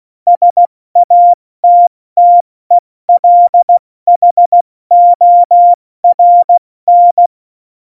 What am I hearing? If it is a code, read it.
Morse code, heard as a keyed tone, about 16 wpm, 700 Hz. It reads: SATTELHORN
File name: Morsekurs / Witze / Witz 11 August 2020, 70 F80